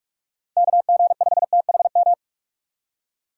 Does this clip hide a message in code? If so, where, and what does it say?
Yes: Morse code from 0.6 to 2.1 s, heard as a keyed tone, about 43 wpm, 700 Hz. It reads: KG5THM